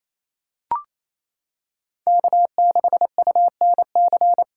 Morse code, heard as a keyed tone, about 28 wpm, 700 Hz. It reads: K6UNC
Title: USA Callsign DE Callsign - 5 Characters - 28wpm